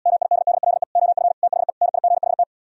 Morse code, heard as a keyed tone, about 78 wpm, 700 Hz. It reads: MIDDLE OF THE NIGHT